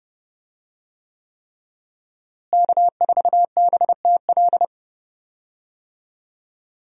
Morse code, heard as a keyed tone, about 32 wpm, 700 Hz. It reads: K4BTL
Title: USA Call Signs - 5 Characters - Repeated 3x - 30wpm